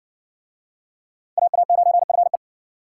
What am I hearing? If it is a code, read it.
Morse code, heard as a keyed tone, about 60 wpm, 700 Hz. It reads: AA9FE